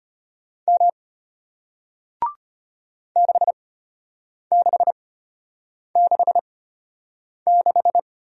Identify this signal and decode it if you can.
Morse code, heard as a keyed tone, about 33 wpm, spaced as for 8 wpm, 700 Hz. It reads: M 6666